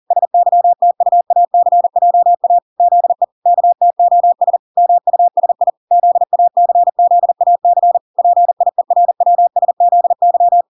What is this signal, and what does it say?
Morse code, heard as a keyed tone, about 40 wpm, 700 Hz. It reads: SYTUACJA ZE KTOS MUSI ZACZAC PIERWSZY
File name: 40wpm